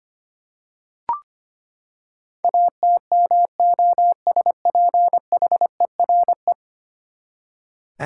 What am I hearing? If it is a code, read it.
Morse code, heard as a keyed tone, about 25 wpm, 700 Hz. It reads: ATMOSPHERE
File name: Ten Letter Words - 25wpm